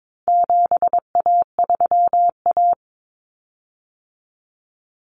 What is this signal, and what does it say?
Morse code, heard as a keyed tone, about 22 wpm, 700 Hz. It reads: 7A3A
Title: Common Cut Numbers - 4-Digit 22wpm